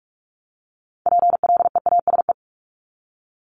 Morse code, heard as a keyed tone, about 46 wpm, 690 Hz. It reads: PLEASE
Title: Multi-letter Phonemes Repeated 3x 45wpm